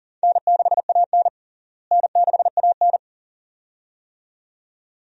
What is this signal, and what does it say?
Morse code, heard as a keyed tone, about 37 wpm, 700 Hz. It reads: N6AN N6AN
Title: USA Call Signs - 4 Characters - Repeated 2x - 40wpm